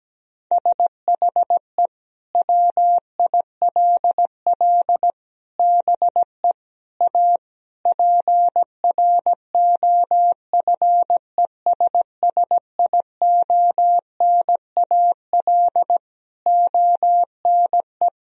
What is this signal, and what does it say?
Morse code, heard as a keyed tone, about 17 wpm, 700 Hz. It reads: SHE WILL BE A PROFESSIONAL ONE
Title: Sentences from Top 1000 Words - Review Part 4 - 17wpm